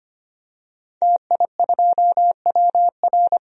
Morse code, heard as a keyed tone, about 25 wpm, 700 Hz. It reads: TI2WR
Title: DXCC-100 Call Signs - 5 Characters - Repeated 3x - 25wpm